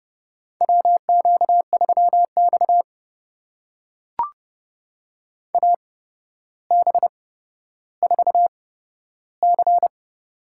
Morse code, heard as a keyed tone, about 30 wpm, spaced as for 13 wpm, 700 Hz. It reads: WQ3X AB4C
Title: USA Call Signs - 8x Character Spacing - 30wpm